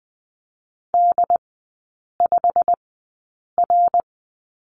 Morse code, heard as a keyed tone, about 19 wpm, spaced as for 8 wpm, 700 Hz. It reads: D5R
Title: 127 Lesson 21e - Number 6 - Three Characters 20wpm